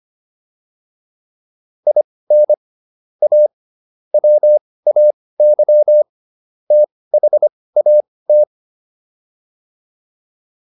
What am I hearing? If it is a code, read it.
Morse code, heard as a keyed tone, about 25 wpm, spaced as for 18 wpm, 600 Hz. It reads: IN A WAY THAT